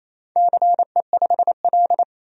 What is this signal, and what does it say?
Morse code, heard as a keyed tone, about 28 wpm, 700 Hz. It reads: CE5L
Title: DXCC-100 Call Signs - 4 Characters - 28wpm